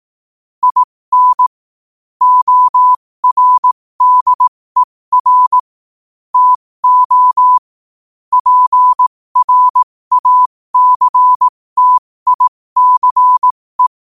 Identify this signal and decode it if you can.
Morse code, heard as a keyed tone, about 18 wpm, spaced as for 15 wpm, 1000 Hz. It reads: IN ORDER TO PRACTICE
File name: qotdfiles / 15wpm